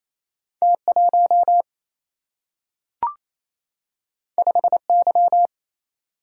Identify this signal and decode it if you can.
Morse code, heard as a keyed tone, about 28 wpm, 700 Hz. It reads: T 1 5 Y
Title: Single-Letter Single-Number 28wpm